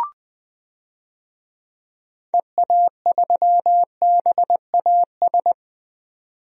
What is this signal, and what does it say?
Morse code, heard as a keyed tone, about 20 wpm, 700 Hz. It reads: EA3BAS
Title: DXCC-100 Call Signs - 6 Characters - 20wpm